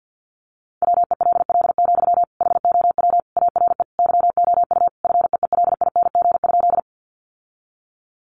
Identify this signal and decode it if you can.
Morse code, heard as a keyed tone, about 50 wpm, 700 Hz. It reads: WELL, HOW ARE YOU FEELING?
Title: Sentences from Top 1200 Words 50wpm